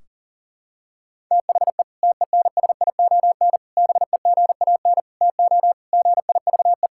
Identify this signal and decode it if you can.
Morse code, heard as a keyed tone, about 40 wpm, 700 Hz. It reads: THE TENSION BEGAN TO GIVE
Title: Sentences from Top 1600 Words 40wpm